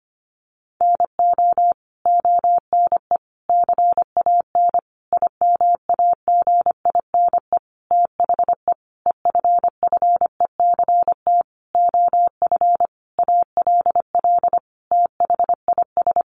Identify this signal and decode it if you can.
Morse code, heard as a keyed tone, about 25 wpm, 700 Hz. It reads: NO ONE CAN IMAGINE THE EFFECT OF ALL THIS
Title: Sentences from Top 1400 Words 25wpm